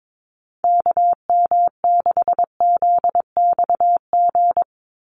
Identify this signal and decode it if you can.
Morse code, heard as a keyed tone, about 22 wpm, 700 Hz. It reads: KM6ZXG
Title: USA Call Signs 22wpm